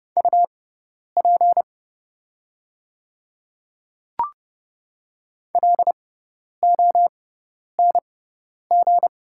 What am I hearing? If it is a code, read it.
Morse code, heard as a keyed tone, about 30 wpm, spaced as for 10 wpm, 690 Hz. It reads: UP LONG
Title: 6x Character Spacing - Top 500 Words - 30wpm